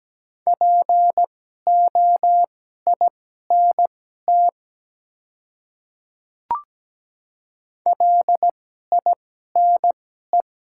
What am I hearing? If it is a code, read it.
Morse code, heard as a keyed tone, about 17 wpm, spaced as for 12 wpm, 700 Hz. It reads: POINT LINE
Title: 2x Character Spacing - Top 200 Words - 17wpm